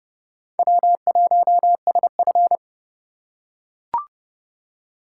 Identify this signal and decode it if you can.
Morse code, heard as a keyed tone, about 29 wpm, 700 Hz. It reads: W1SF